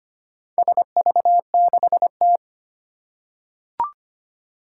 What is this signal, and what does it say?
Morse code, heard as a keyed tone, about 25 wpm, 700 Hz. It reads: SV6T
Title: Four Challenging Letter-Number Combinations 25wpm